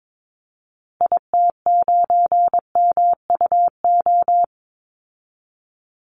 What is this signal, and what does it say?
Morse code, heard as a keyed tone, about 22 wpm, 700 Hz. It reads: IT9MUO
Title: Call Signs Repeated 2x - 22wpm